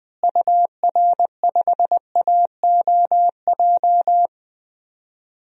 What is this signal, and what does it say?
Morse code, heard as a keyed tone, about 20 wpm, 700 Hz. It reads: UR5AOJ
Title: DXCC-100 Call Signs - 6 Characters - Repeated 2x - 20wpm